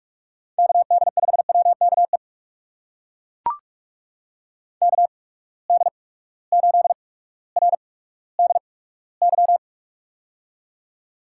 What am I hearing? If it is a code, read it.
Morse code, heard as a keyed tone, about 42 wpm, spaced as for 19 wpm, 700 Hz. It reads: KD5WKE KD8RDY
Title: USA Call Signs - 6 Characters - 8x Character Spacing - 45wpm